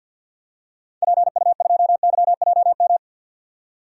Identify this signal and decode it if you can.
Morse code, heard as a keyed tone, about 50 wpm, 700 Hz. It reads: PU2YJM
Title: DXCC-100 Call Signs - 6 Characters - 50wpm